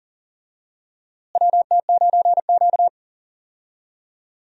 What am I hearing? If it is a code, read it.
Morse code, heard as a keyed tone, about 40 wpm, 700 Hz. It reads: WT9Q
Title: USA Call Signs - 4 Characters - Repeated 3x - 40wpm